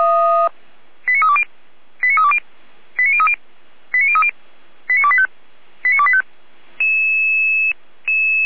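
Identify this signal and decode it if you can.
Five-tone selective call (ZVEI), heard as quick runs of five tones, tones of about 80 ms. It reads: 89320 89320 89030 89030 89285 89285
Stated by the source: ZVEI five-tone selective call received on SDR.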